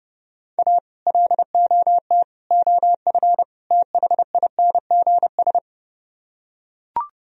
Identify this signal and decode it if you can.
Morse code, heard as a keyed tone, about 30 wpm, 700 Hz. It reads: A LOT OF THINGS